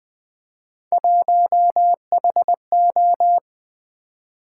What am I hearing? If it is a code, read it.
Morse code, heard as a keyed tone, about 20 wpm, 700 Hz. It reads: 1HO
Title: Three Letter-Number Combinations - 20wpm